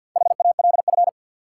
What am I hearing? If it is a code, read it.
Morse code, heard as a keyed tone, about 48 wpm, 700 Hz. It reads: HALF
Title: Sentences from Top 400 Words - New Word Drill - 50wpm